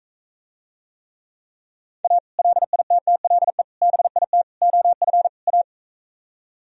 Morse code, heard as a keyed tone, about 44 wpm, 700 Hz. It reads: A LITTLE BIT OF A